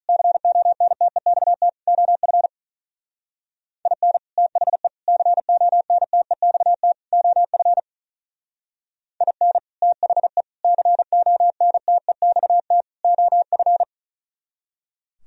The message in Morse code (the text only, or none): CONTEXT OF IN THE CONTEXT OF IN THE CONTEXT OF